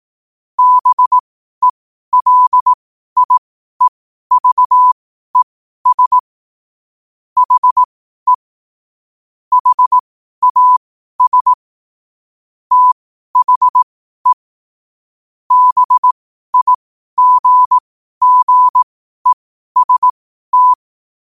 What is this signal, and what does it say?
Morse code, heard as a keyed tone, about 18 wpm, spaced as for 12 wpm, 1000 Hz. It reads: BELIEVES HE HAS THE BIGGEST
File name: 12wpm